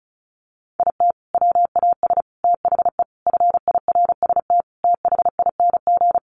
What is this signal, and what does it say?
Morse code, heard as a keyed tone, about 34 wpm, 700 Hz. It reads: IT WAS THE FIRST THING
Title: Sentences from Top 300 Words 35wpm